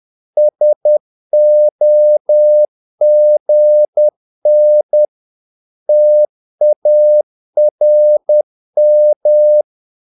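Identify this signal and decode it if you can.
Morse code, heard as a keyed tone, about 10 wpm, 600 Hz. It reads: SOGN TARM
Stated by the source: Morse practice of short Danish words at 10 wpm, from CW med Gnister.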